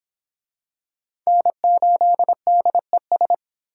Morse code, heard as a keyed tone, about 22 wpm, 700 Hz. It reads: N8DES